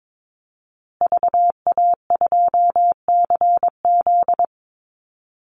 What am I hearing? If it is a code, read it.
Morse code, heard as a keyed tone, about 22 wpm, 700 Hz. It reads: VA2CZ
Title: Call Signs Repeated 3x - 22wpm